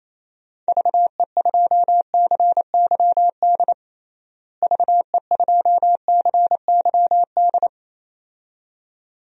Morse code, heard as a keyed tone, about 28 wpm, 700 Hz. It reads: VE2CYD VE2CYD
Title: DXCC-100 Call Signs - 6 Characters - Repeated 2x - 28wpm